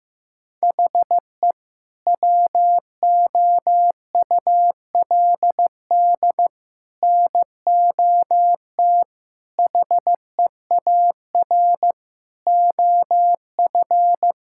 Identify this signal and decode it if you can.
Morse code, heard as a keyed tone, about 16 wpm, 700 Hz. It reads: HE WOULD NOT HEAR OF
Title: Sentences from Top 1000 Words - Review Part 5 - 15wpm